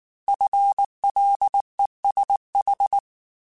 Morse code, heard as a keyed tone, about 19 wpm, 800 Hz. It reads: FLESH
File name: FL Runs / FL Runs mp3s / FL Runs @18